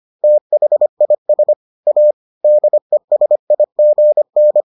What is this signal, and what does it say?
Morse code, heard as a keyed tone, about 25 wpm, 600 Hz. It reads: THIS A DESIGN